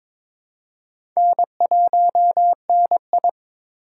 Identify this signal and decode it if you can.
Morse code, heard as a keyed tone, about 21 wpm, 700 Hz. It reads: N1NI
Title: USA Call Signs - 4 Characters - 2x Character Spacing - 22wpm